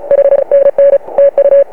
Morse code, heard as a keyed tone, about 36 wpm, 560 Hz. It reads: H NN TU